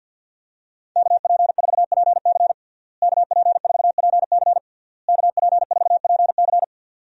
Morse code, heard as a keyed tone, about 53 wpm, 700 Hz. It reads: KP4PC KP4PC KP4PC
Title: DXCC-100 Call Signs - 5 Characters - Repeated 3x - 50wpm